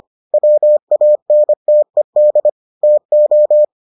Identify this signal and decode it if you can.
Morse code, heard as a keyed tone, about 25 wpm, 600 Hz. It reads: WANTED TO